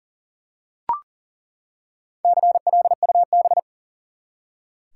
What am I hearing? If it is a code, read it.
Morse code, heard as a keyed tone, about 40 wpm, 700 Hz. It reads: CLUB
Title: Four Letter Words - 40wpm